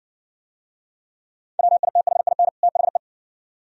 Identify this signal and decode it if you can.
Morse code, heard as a keyed tone, about 55 wpm, 700 Hz. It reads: WITHIN THE